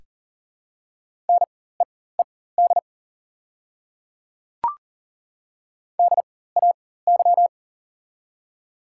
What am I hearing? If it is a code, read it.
Morse code, heard as a keyed tone, about 40 wpm, spaced as for 19 wpm, 690 Hz. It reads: NEED DAY
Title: 4x Character Spacing - Top 200 Words - 40wpm